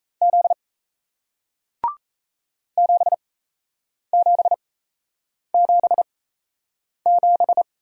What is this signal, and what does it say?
Morse code, heard as a keyed tone, about 37 wpm, spaced as for 8 wpm, 700 Hz. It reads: Z 7777